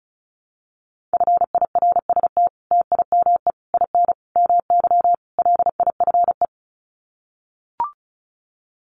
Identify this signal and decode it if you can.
Morse code, heard as a keyed tone, about 34 wpm, 700 Hz. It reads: FIRST TIME IN MY LIFE
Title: Sets of 5 Words Encore2 35wpm